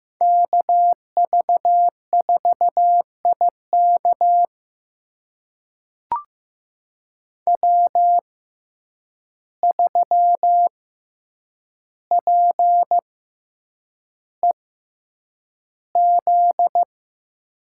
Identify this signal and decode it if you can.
Morse code, heard as a keyed tone, about 15 wpm, spaced as for 8 wpm, 700 Hz. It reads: KV4IK W3PEZ
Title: USA Call Signs - 5 Characters - 6x Character Spacing - 15wpm